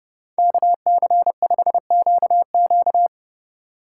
Morse code, heard as a keyed tone, about 37 wpm, 700 Hz. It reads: KC5QQ
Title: Speed Racing - USA 2x2 Call Signs 30wpm